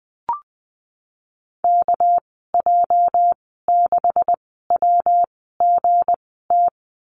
Morse code, heard as a keyed tone, about 20 wpm, spaced as for 15 wpm, 700 Hz. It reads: KJ6WGT
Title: 207 Lesson 33g - Letter j - Call Signs 20wpm